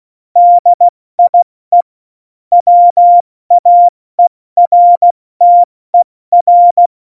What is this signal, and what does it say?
Morse code, heard as a keyed tone, about 16 wpm, spaced as for 14 wpm, 700 Hz. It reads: DIE WAERTER
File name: Morsekurs / Witze / Witz 27. Februar 2018, 70 F80